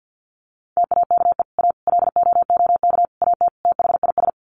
Morse code, heard as a keyed tone, about 50 wpm, 700 Hz. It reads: TAKE A LOOK AT THIS
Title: Sets of 5 Words 50wpm